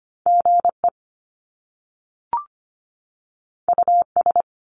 Morse code, heard as a keyed tone, about 25 wpm, 700 Hz. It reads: G E U S
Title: Top Two Letter Combinations in Words 25wpm